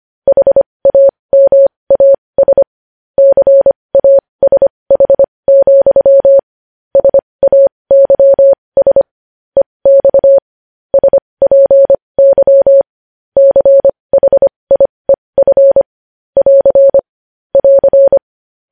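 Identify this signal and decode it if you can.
Morse code, heard as a keyed tone, about 25 wpm, spaced as for 20 wpm, 550 Hz. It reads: HAMAS CASH, SAYS EX SPY CHIEF <AR> <AR>
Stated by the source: Morse code news headlines sent at 20.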